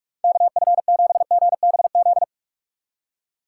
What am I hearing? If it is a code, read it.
Morse code, heard as a keyed tone, about 44 wpm, 700 Hz. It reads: K F 7 G B Z
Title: USA Call Signs - 6 Characters - 6x Character Spacing - 45wpm